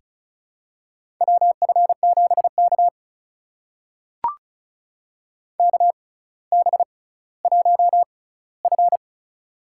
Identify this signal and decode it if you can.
Morse code, heard as a keyed tone, about 36 wpm, spaced as for 18 wpm, 700 Hz. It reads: WF7K KB1F
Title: USA Call Signs - 4 Characters - 6x Character Spacing - 35wpm